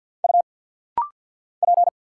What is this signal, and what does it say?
Morse code, heard as a keyed tone, about 50 wpm, 700 Hz. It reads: U P
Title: Single Letter-Number Alt - Rapid-Fire 50wpm